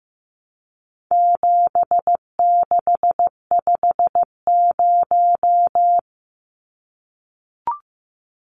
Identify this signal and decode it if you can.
Morse code, heard as a keyed tone, about 15 wpm, 700 Hz. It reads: 7650
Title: Numbers - 4-Digit 15wpm